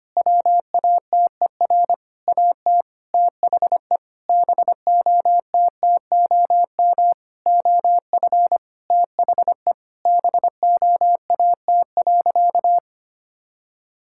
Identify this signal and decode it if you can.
Morse code, heard as a keyed tone, about 25 wpm, 700 Hz. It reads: WATER AT THE BOTTOM OF THE BOAT.
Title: Sentences from Top 1000 Words - Review Part 6 - 25wpm